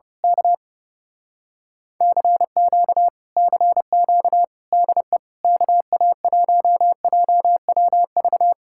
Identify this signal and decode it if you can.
Morse code, heard as a keyed tone, about 30 wpm, 700 Hz. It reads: K CQ CQ DE KA1JWV
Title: Speed-Racing Common QSO Phrases - 30wpm